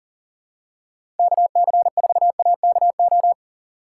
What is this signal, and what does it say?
Morse code, heard as a keyed tone, about 37 wpm, 700 Hz. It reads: KC4AKO
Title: USA Call Signs - 6 Characters - 4x Character Spacing - 40wpm